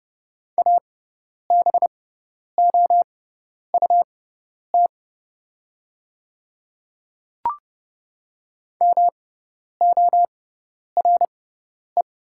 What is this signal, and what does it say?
Morse code, heard as a keyed tone, about 30 wpm, spaced as for 10 wpm, 700 Hz. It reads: ABOUT MORE
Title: 6x Character Spacing - Top 100 Words - 30wpm